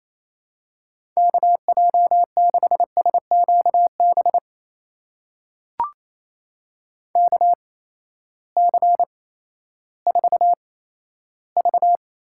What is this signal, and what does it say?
Morse code, heard as a keyed tone, about 28 wpm, spaced as for 14 wpm, 700 Hz. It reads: KJ6SQB KC4V
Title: USA Call Signs - 6 Characters - 8x Character Spacing - 28wpm